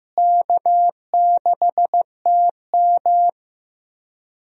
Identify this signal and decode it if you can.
Morse code, heard as a keyed tone, about 15 wpm, 700 Hz. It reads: K6TM
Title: USA Call Signs - 4 Characters - Repeated 2x - 15wpm